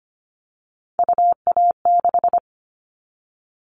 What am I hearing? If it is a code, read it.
Morse code, heard as a keyed tone, about 25 wpm, 700 Hz. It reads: UA6
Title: Callsign Prefixes for Top 100 DXCC - 25wpm